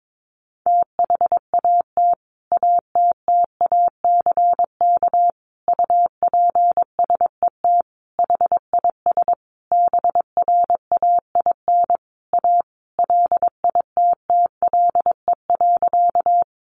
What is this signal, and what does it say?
Morse code, heard as a keyed tone, about 22 wpm, 700 Hz. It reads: THAT ATTACK UPSET HIS BRAIN A LITTLE.
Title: Sentences from Top 1500 Words 22wpm